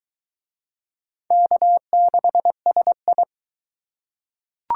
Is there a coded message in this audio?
Yes: Morse code from 1.3 to 3.2 s, heard as a keyed tone, about 22 wpm, 700 Hz. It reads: K6SI